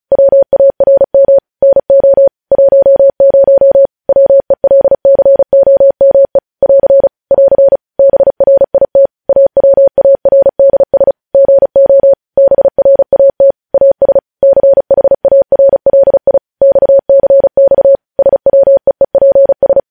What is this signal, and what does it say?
Morse code, heard as a keyed tone, about 36 wpm, 550 Hz. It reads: WARM NO 10 WELCOME <AR> <AR> BRIT AWARDS GO BRAT AS CHARLI XCX SWEEPS